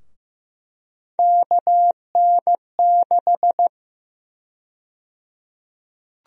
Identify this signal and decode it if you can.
Morse code, heard as a keyed tone, about 15 wpm, 700 Hz. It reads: KN6
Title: USA Call Sign Prefixes - Repeated 2x - 15wpm